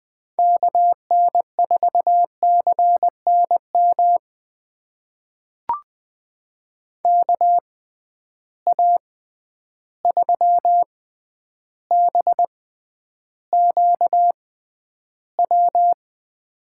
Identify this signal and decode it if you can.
Morse code, heard as a keyed tone, about 20 wpm, spaced as for 10 wpm, 700 Hz. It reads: KN4CNM KA3BQW
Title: USA Call Signs - 6 Characters - 6x Character Spacing - 20wpm